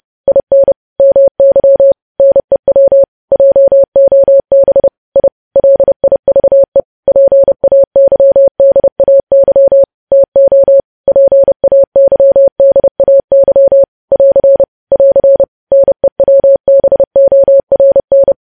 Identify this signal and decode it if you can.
Morse code, heard as a keyed tone, about 30 wpm, 550 Hz. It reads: IN MY NEW JOB I LIVE PAYDAY TO PAYDAY <AR> <AR> NEWBORN
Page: news headlines in morse code at 30 wpm